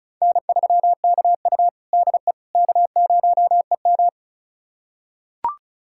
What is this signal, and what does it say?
Morse code, heard as a keyed tone, about 36 wpm, 700 Hz. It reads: N3KU DE K0EM
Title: USA Callsign DE Callsign - 4 Characters - 35wpm